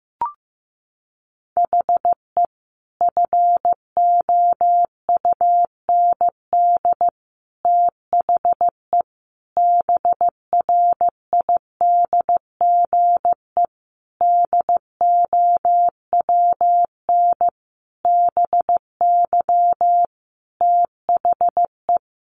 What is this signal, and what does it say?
Morse code, heard as a keyed tone, about 14 wpm, 700 Hz. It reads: HE FOUND THE BRIDGE DOWN BY THE
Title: Sentences from Top 1200 Words 15wpm